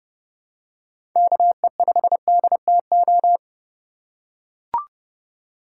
Morse code, heard as a keyed tone, about 29 wpm, 700 Hz. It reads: KE5DTO